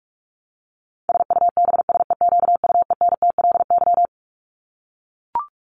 Morse code, heard as a keyed tone, about 44 wpm, 700 Hz. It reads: SUBSEQUENTLY
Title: LY Suffix - Top 100 Words with Base Words 45wpm